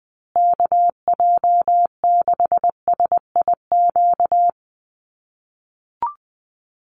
Morse code, heard as a keyed tone, about 20 wpm, 700 Hz. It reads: KJ6SIQ
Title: USA Call Signs Repeated 2x 20wpm